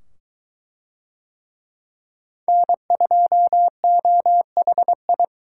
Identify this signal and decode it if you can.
Morse code, heard as a keyed tone, about 23 wpm, 700 Hz. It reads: N2OHI